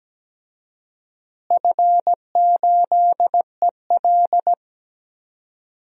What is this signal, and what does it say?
Morse code, heard as a keyed tone, about 17 wpm, 700 Hz. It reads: F8EL
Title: DXCC-100 Call Signs - 4 Characters - Repeated 3x - 17wpm